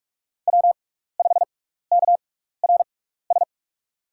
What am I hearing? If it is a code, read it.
Morse code, heard as a keyed tone, about 45 wpm, spaced as for 15 wpm, 700 Hz. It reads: W5KRS
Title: USA Call Signs - 6x Character Spacing - 45wpm